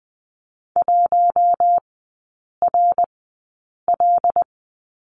Morse code, heard as a keyed tone, about 20 wpm, spaced as for 8 wpm, 700 Hz. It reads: 1RL
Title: 074 Lesson 13e - Letter L - Three Characters 20wpm